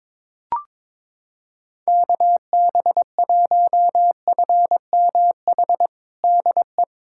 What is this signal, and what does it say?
Morse code, heard as a keyed tone, about 22 wpm, 700 Hz. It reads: KB1FMH DE
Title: USA Callsign DE Callsign - 6 Characters - 22wpm